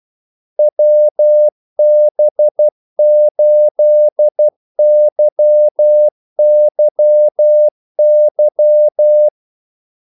Morse code, heard as a keyed tone, about 12 wpm, 600 Hz. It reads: WB8YYY
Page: 2 ekstra ordmellemrum mellem kaldesignalerne